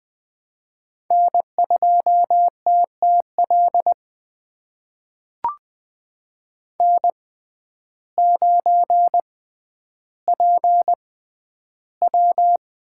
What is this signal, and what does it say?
Morse code, heard as a keyed tone, about 20 wpm, spaced as for 11 wpm, 700 Hz. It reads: N2TTL N9PW
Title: USA Call Signs - 5 Characters - 6x Character Spacing - 20wpm